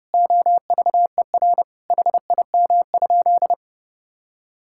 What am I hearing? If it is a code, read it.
Morse code, heard as a keyed tone, about 30 wpm, 700 Hz. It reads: OVER HIM?
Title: Sentences from Top 1600 Words 30wpm